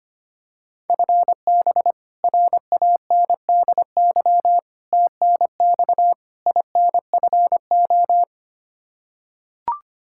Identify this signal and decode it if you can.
Morse code, heard as a keyed tone, about 25 wpm, 700 Hz. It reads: FB RANDY TNX INFO
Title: Practice QSO - Exchange 3 - More Info - 25wpm